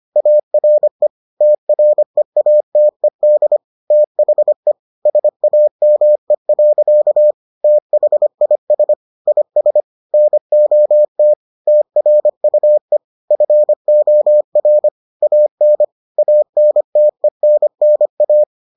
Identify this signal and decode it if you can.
Morse code, heard as a keyed tone, about 25 wpm, 600 Hz. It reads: ARE TREATED THE SAME. THIS IS NOT TRUE FOR AN ANTENNA